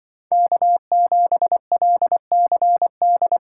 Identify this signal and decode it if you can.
Morse code, heard as a keyed tone, about 24 wpm, 700 Hz. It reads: K7LCD
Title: Speed Racing - USA 1x3 Call Signs 20wpm